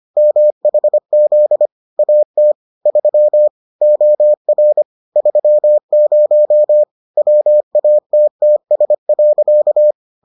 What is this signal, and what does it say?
Morse code, heard as a keyed tone, about 25 wpm, 600 Hz. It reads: MHZ AT 3 OR 30 WATTS.